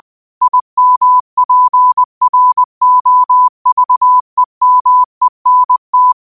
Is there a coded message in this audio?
Yes: Morse code from 0.4 to 6.1 s, heard as a keyed tone, about 20 wpm, 1000 Hz. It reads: IMPROVEMENT